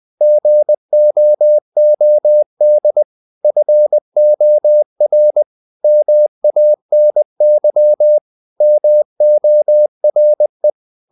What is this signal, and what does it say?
Morse code, heard as a keyed tone, about 20 wpm, 600 Hz. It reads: GOOD FOR MANY MORE
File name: the old man and the sea part 2 20wpm